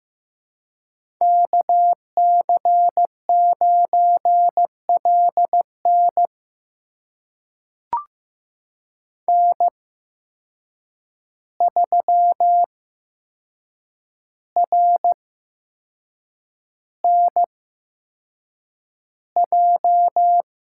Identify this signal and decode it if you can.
Morse code, heard as a keyed tone, about 15 wpm, spaced as for 6 wpm, 700 Hz. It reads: KC9LN N3RNJ
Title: USA Call Signs - 5 Characters - 8x Character Spacing - 15wpm